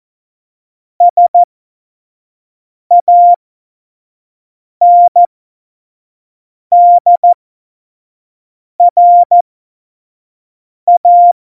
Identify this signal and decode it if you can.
Morse code, heard as a keyed tone, about 14 wpm, spaced as for 5 wpm, 700 Hz. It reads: SANDRA
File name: Morsekurs / Witze / Witz 11. Juni 2019, 25 F70